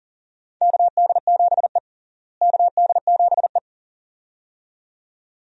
Morse code, heard as a keyed tone, about 40 wpm, 700 Hz. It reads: KD7E KD7E
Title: USA Call Signs - 4 Characters - Repeated 2x - 40wpm